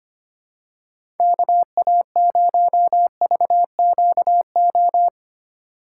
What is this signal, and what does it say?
Morse code, heard as a keyed tone, about 25 wpm, 700 Hz. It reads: KA0VQO
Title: USA Call Signs - 6 Characters - 3x Character Spacing - 25wpm